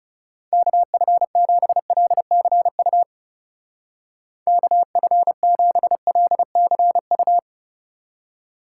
Morse code, heard as a keyed tone, about 36 wpm, 700 Hz. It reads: KF7LCU KF7LCU
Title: Speed Racing - USA Call Signs 30wpm